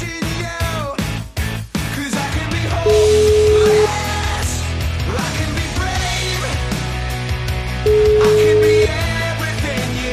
Ringback tone (Europe, 425 Hz), with noise.